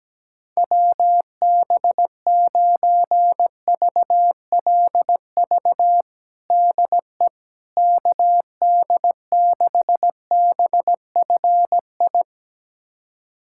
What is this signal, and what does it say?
Morse code, heard as a keyed tone, about 17 wpm, 700 Hz. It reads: WB9VLV DE KD6BFI
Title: USA Callsign DE Callsign 17wpm